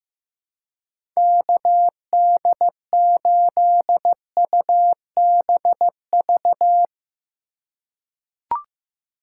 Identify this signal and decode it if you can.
Morse code, heard as a keyed tone, about 15 wpm, 700 Hz. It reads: KD8UBV
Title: USA Call Signs - 6 Characters - 4x Character Spacing - 15wpm